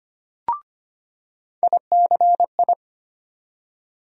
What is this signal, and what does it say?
Morse code, heard as a keyed tone, about 25 wpm, 700 Hz. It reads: ICI
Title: USA Call Sign Postfixes - 3 Letters 25wpm